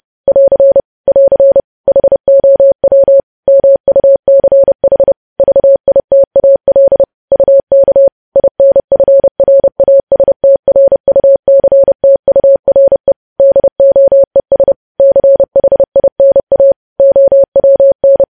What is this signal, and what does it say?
Morse code, heard as a keyed tone, about 30 wpm, 550 Hz. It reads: <AR> <AR> HOW MUCH VITAL UK INFRASTRUCTURE DOES CHINA OWN
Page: news headlines in morse code at 30 wpm